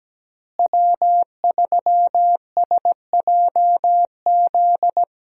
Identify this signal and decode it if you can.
Morse code, heard as a keyed tone, about 17 wpm, 700 Hz. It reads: W3SJZ
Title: USA Call Signs - 5 Characters - Repeated 2x - 17wpm